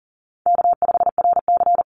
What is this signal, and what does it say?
Morse code, heard as a keyed tone, about 40 wpm, 700 Hz. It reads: K5RC
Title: Call Signs - 40wpm